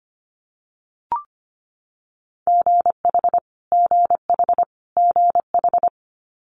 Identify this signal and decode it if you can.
Morse code, heard as a keyed tone, about 25 wpm, 700 Hz. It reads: GH GH GH
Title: Multi-letter Phonemes Repeated 3x 25wpm